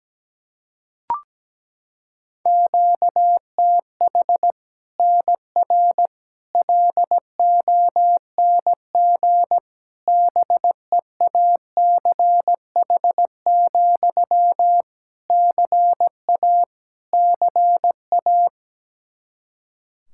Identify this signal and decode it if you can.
Morse code, heard as a keyed tone, about 17 wpm, 700 Hz. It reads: QTH NR LONG BEACH, CA CA
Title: K1USN SST QSO Practice 17wpm